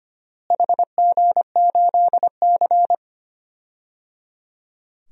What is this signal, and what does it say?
Morse code, heard as a keyed tone, about 25 wpm, 700 Hz. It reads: HG8C
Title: DXCC-100 Call Signs - 4 Characters - Repeated 3x - 25wpm